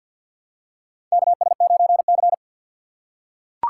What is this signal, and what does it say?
Morse code, heard as a keyed tone, about 47 wpm, 700 Hz. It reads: KS9C
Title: USA Call Signs - 4 Characters - 6x Character Spacing - 50wpm